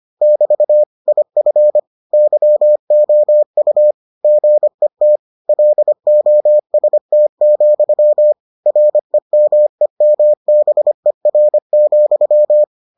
Morse code, heard as a keyed tone, about 25 wpm, 600 Hz. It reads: <BT> IF YOU GET LOST, REMEMBER,